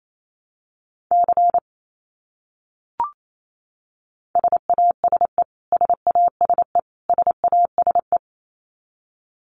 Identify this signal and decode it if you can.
Morse code, heard as a keyed tone, about 28 wpm, 700 Hz. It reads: C SASE SASE SASE
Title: Abbreviations Repeated 3x - 28wpm